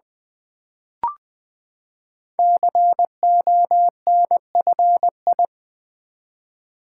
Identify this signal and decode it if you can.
Morse code, heard as a keyed tone, about 20 wpm, 700 Hz. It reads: CONFI
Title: Build A Word - 5 Letters At A Time - 12-Letter Words - 20wpm